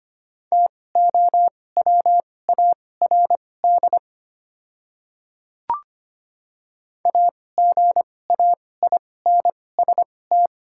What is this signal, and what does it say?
Morse code, heard as a keyed tone, about 25 wpm, spaced as for 18 wpm, 700 Hz. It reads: TOWARD AGAINST
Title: 2x Character Spacing - Top 500 Words - 25wpm